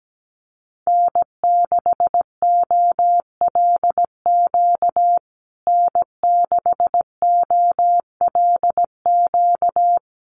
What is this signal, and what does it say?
Morse code, heard as a keyed tone, about 17 wpm, 700 Hz. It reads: N6OLQ N6OLQ
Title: USA Call Signs Repeated 3x 17wpm